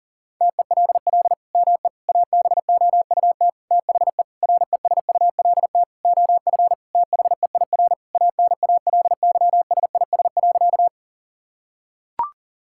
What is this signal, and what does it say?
Morse code, heard as a keyed tone, about 40 wpm, 700 Hz. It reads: TELL ME ABOUT THE RESULT OF THEIR ANALYSIS.
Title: Sentences from Top 1000 Words - Review Part 5 - 40wpm